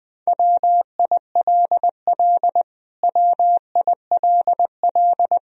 Morse code, heard as a keyed tone, about 20 wpm, 700 Hz. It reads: WILL WILL
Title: Top 100 Words Repeated 20wpm